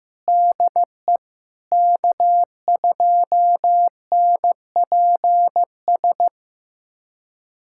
Morse code, heard as a keyed tone, about 16 wpm, 700 Hz. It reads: DE K2NPS
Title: USA Callsign DE Callsign 15wpm